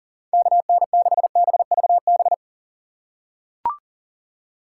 Morse code, heard as a keyed tone, about 37 wpm, 700 Hz. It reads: KN6BVB
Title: USA Call Signs - 6 Characters - 6x Character Spacing - 40wpm